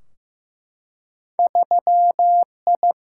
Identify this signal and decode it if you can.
Morse code, heard as a keyed tone, about 15 wpm, 700 Hz. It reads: 3I
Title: Single-Letter Single-Number 15wpm